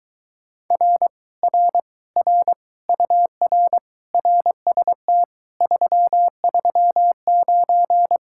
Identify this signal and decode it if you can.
Morse code, heard as a keyed tone, about 23 wpm, 700 Hz. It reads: R R R UR RST 339